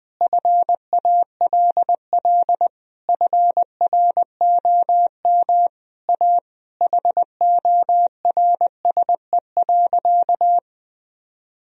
Morse code, heard as a keyed tone, about 20 wpm, 700 Hz. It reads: FALL FROM A HORSE.
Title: Sentences from Top 1000 Words - Review Part 2 - 20wpm